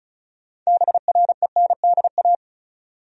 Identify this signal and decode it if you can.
Morse code, heard as a keyed tone, about 36 wpm, 700 Hz. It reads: BRENDA
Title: USA Common Names - 35wpm